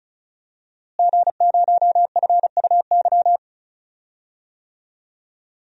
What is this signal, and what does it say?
Morse code, heard as a keyed tone, about 34 wpm, 700 Hz. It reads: G0FUY
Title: DXCC-100 Call Signs - 5 Characters - Repeated 2x - 35wpm